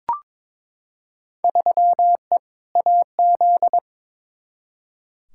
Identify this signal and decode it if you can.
Morse code, heard as a keyed tone, about 22 wpm, 700 Hz. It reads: 3E AZ
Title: Field Day Exchanges - 22wpm